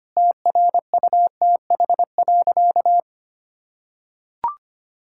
Morse code, heard as a keyed tone, about 25 wpm, 700 Hz. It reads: TRUTH.
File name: practice / 25wpm / Sentences with 9-Word Phrases - 25wpm